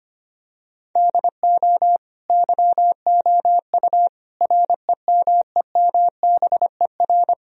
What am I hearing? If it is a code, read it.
Morse code, heard as a keyed tone, about 25 wpm, 700 Hz. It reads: DO YOU REMEMBER
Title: Sentences from Top 1000 Words - Review Part 4 - 25wpm